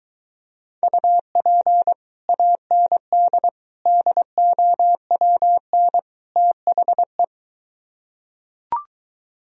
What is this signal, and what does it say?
Morse code, heard as a keyed tone, about 23 wpm, 700 Hz. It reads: UP AND DOWN THE